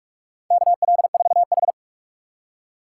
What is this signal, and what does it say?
Morse code, heard as a keyed tone, about 47 wpm, 700 Hz. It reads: KL4H
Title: USA Call Signs - 4 Characters - 6x Character Spacing - 45wpm